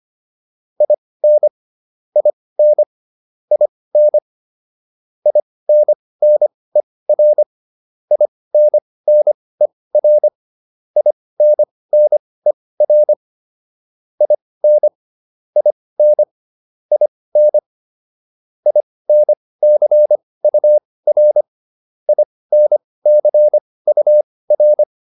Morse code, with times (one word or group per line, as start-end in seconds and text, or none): 0.8-1.5 IN
2.2-2.8 IN
3.5-4.2 IN
5.3-7.4 INNER
8.1-10.3 INNER
11.0-13.1 INNER
14.2-14.9 IN
15.6-16.2 IN
16.9-17.6 IN
18.7-21.4 INCUR
22.1-24.8 INCUR